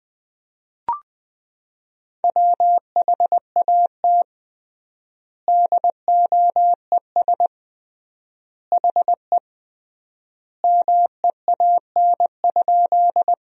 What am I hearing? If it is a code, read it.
Morse code, heard as a keyed tone, about 21 wpm, 700 Hz. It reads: WHAT DOES HE MEAN?
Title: 3x Word-Spacing - Sentences from Top 200 Words - 20wpm